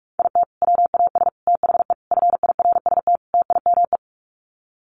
Morse code, heard as a keyed tone, about 44 wpm, 700 Hz. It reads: IT WAS THE FIRST TIME
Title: Sets of 5 Words 45wpm